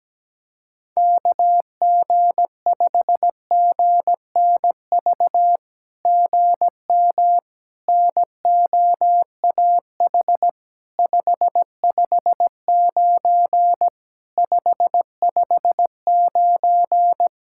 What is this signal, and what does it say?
Morse code, heard as a keyed tone, about 17 wpm, 700 Hz. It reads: KG5GNV GM NOAH 559 559
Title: SOTA QSO Practice 17wpm